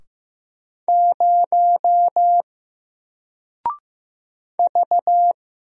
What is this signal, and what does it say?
Morse code, heard as a keyed tone, about 15 wpm, 700 Hz. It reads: 0 V
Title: Single Character (Letter:Number:Punctuation:Prosign) Alt 15wpm